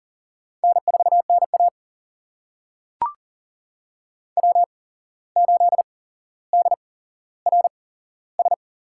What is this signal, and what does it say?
Morse code, heard as a keyed tone, about 40 wpm, spaced as for 16 wpm, 700 Hz. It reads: N4NA W8DRS
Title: USA Call Signs - 8x Character Spacing - 40wpm